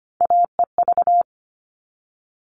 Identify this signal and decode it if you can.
Morse code, heard as a keyed tone, about 25 wpm, 700 Hz. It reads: AEV
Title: All Cut Numbers - 3-Digit 25wpm